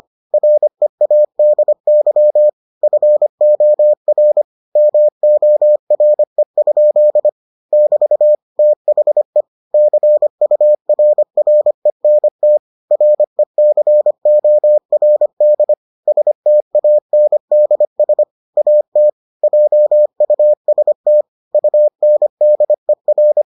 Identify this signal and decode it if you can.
Morse code, heard as a keyed tone, about 25 wpm, 600 Hz. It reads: READY FOR MORE? <BT> THE CURRENT RECORD STANDS AT JUST UNDER